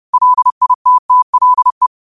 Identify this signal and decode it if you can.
Morse code, heard as a keyed tone, about 30 wpm, 1000 Hz. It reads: LITTLE
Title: 30 WPM morse code quotes for Fri, 01 Mar 2024